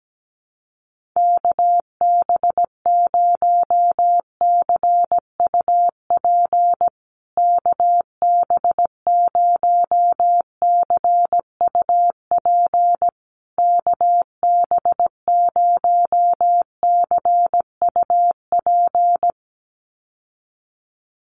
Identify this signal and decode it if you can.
Morse code, heard as a keyed tone, about 17 wpm, 700 Hz. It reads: KB0CUP KB0CUP KB0CUP
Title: USA Call Signs Repeated 3x 17wpm